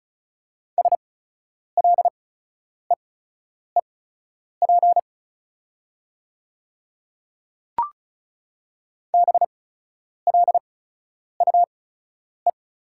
Morse code, heard as a keyed tone, about 36 wpm, spaced as for 10 wpm, 690 Hz. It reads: SLEEP BLUE
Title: 8x Character Spacing - Top 500 Words - 35wpm